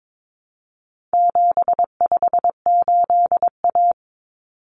Morse code, heard as a keyed tone, about 22 wpm, 700 Hz. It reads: 758A
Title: Common Cut Numbers - 4-Digit 22wpm